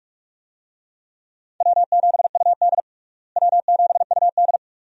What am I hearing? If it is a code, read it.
Morse code, heard as a keyed tone, about 45 wpm, 700 Hz. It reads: W7UD W7UD
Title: USA Call Signs - 4 Characters - Repeated 2x - 45wpm